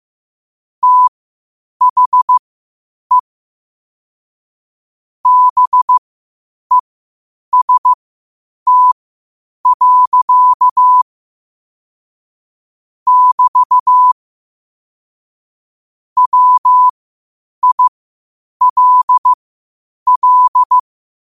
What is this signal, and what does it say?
Morse code, heard as a keyed tone, about 15 wpm, spaced as for 8 wpm, 1000 Hz. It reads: THE BEST. <BT> WILL
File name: qotdfiles / 8wpm